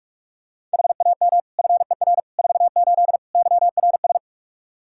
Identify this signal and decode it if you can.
Morse code, heard as a keyed tone, about 47 wpm, 700 Hz. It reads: HAM FER 48 YRS
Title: Practice QSO - Exchange 3 - More Info - 45wpm